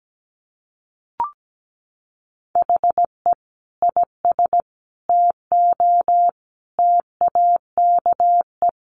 Morse code, heard as a keyed tone, about 17 wpm, 700 Hz. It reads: HE IS TO TAKE
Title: Sentences from Top 1400 Words 17wpm